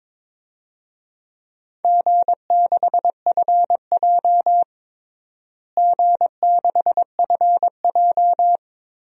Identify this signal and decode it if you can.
Morse code, heard as a keyed tone, about 22 wpm, 700 Hz. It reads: G6FJ G6FJ
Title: DXCC-100 Call Signs - 4 Characters - Repeated 3x - 22wpm